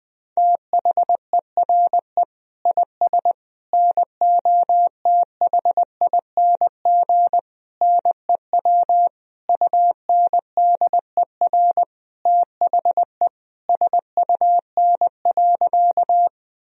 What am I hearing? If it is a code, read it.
Morse code, heard as a keyed tone, about 20 wpm, 700 Hz. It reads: THERE IS NOTHING NEW UNDER THE SUN.
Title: Sentences from Top 1000 Words - Review Part 1 - 20wpm